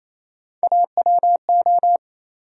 Morse code, heard as a keyed tone, about 28 wpm, 700 Hz. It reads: AWO
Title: USA Call Sign Postfixes - 3 Letters 28wpm